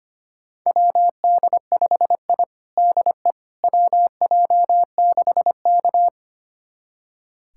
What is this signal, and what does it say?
Morse code, heard as a keyed tone, about 25 wpm, 700 Hz. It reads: WD5I DE WJ6K
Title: USA Callsign DE Callsign - 4 Characters - 25wpm